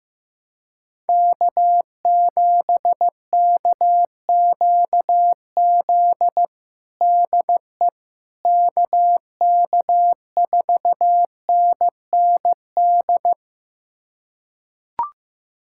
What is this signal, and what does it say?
Morse code, heard as a keyed tone, about 15 wpm, 700 Hz. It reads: K7KQZ DE KK4NND
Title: USA Callsign DE Callsign 15wpm